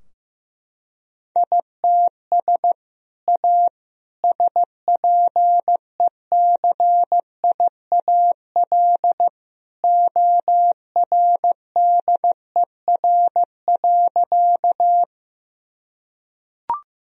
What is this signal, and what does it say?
Morse code, heard as a keyed tone, about 16 wpm, 700 Hz. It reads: ITS A SPECIAL ORDER.
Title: Sentences from Top 1000 Words - Review Part 3 - 15wpm